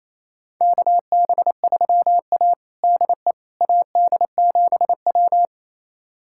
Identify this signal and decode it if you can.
Morse code, heard as a keyed tone, about 28 wpm, 700 Hz. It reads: KB3A DE AD7W
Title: USA Callsign DE Callsign - 4 Characters - 28wpm